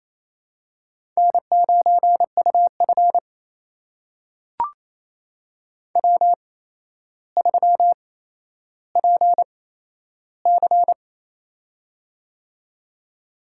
Morse code, heard as a keyed tone, about 28 wpm, spaced as for 12 wpm, 700 Hz. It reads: N9UF W3PC
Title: USA Call Signs - 8x Character Spacing - 28wpm